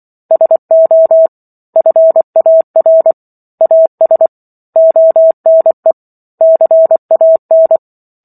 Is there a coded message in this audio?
Yes: Morse code from 0.3 to 7.8 s, heard as a keyed tone, about 24 wpm, 650 Hz. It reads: SO FAR AS ONE CAN